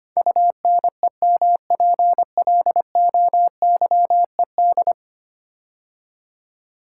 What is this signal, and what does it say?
Morse code, heard as a keyed tone, about 25 wpm, 700 Hz. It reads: UNEMPLOYED
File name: 10 Letter Words - First 3 Letters Given - 25wpm